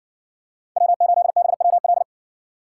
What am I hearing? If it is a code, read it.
Morse code, heard as a keyed tone, about 57 wpm, 700 Hz. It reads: W8BWL